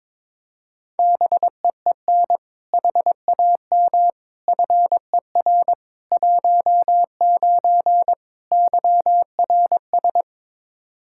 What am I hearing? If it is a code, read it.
Morse code, heard as a keyed tone, about 22 wpm, 700 Hz. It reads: BEEN HAM FER 19 YRS
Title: Practice QSO - Exchange 3 - More Info - 22wpm